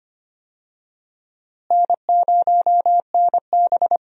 Morse code, heard as a keyed tone, about 25 wpm, 700 Hz. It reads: N0NB
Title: USA Call Signs - 4 Characters - Repeated 2x - 25wpm